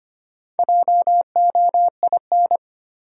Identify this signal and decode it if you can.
Morse code, heard as a keyed tone, about 25 wpm, 700 Hz. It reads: JOIN
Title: Sentences from Top 1400 Words - New Word Drill Alt - 25wpm